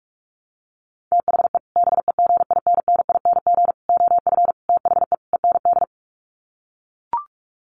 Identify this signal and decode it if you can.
Morse code, heard as a keyed tone, about 44 wpm, 700 Hz. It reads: THE BEGINNING OF THE END
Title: Sets of 5 Words Encore2 45wpm